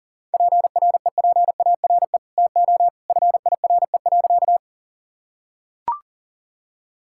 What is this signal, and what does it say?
Morse code, heard as a keyed tone, about 40 wpm, 700 Hz. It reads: PREPARE TO FIRE.
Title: Sentences from Top 1000 Words - Review Part 6 - 40wpm